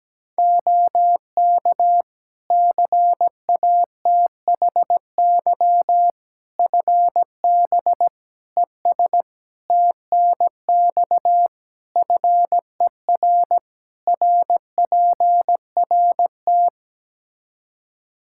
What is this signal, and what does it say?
Morse code, heard as a keyed tone, about 17 wpm, 700 Hz. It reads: OK CATHY FB ES TNX FER RPRT
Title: K1USN SST QSO Practice 17wpm